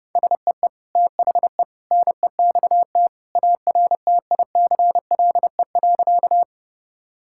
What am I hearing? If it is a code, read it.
Morse code, heard as a keyed tone, about 30 wpm, 700 Hz. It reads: SEE THE NEXT ARTICLE.
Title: Sentences from Top 1000 Words - Review Part 5 - 30wpm